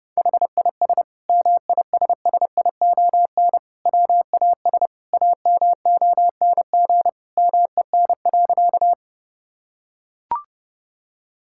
Morse code, heard as a keyed tone, about 30 wpm, 700 Hz. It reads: HIS MISSION WAS AMONG MEN.
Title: Sentences from Top 1600 Words 30wpm